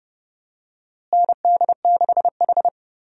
Morse code, heard as a keyed tone, about 30 wpm, 700 Hz. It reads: ND6H
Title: USA Call Signs - 4 Characters - Repeated 2x - 30wpm